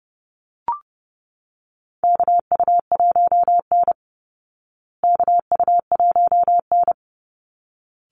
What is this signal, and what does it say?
Morse code, heard as a keyed tone, about 30 wpm, 700 Hz. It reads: KU1N KU1N
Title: Call Signs Repeated 3x - 30wpm